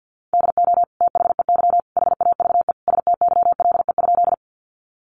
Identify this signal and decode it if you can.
Morse code, heard as a keyed tone, about 50 wpm, 700 Hz. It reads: DO THEY HAVE STYLE?
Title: Sentences from Top 800 Words 50wpm